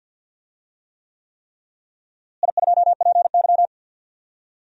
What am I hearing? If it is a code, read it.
Morse code, heard as a keyed tone, about 48 wpm, 700 Hz. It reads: I2PY
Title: DXCC-100 Call Signs - 4 Characters - Repeated 3x - 50wpm